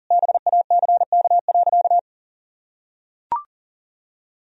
Morse code, heard as a keyed tone, about 40 wpm, 700 Hz. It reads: BACK.
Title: Sentences from Top 1000 Words - Review Part 1 - 40wpm